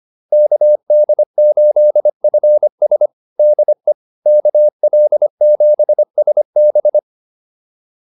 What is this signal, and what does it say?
Morse code, heard as a keyed tone, about 25 wpm, 600 Hz. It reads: KD8FS DE KL7SB